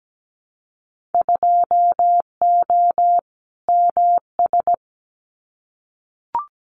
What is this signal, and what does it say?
Morse code, heard as a keyed tone, about 17 wpm, 700 Hz. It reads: 2O MS
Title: Winter Field Day Exchanges 17wpm